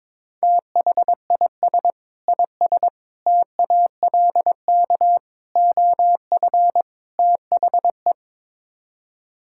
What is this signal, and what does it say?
Morse code, heard as a keyed tone, about 21 wpm, 700 Hz. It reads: THIS IS TALK OF THE
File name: practice / 22wpm / Build A Phrase - 5-Word Phrases - 22wpm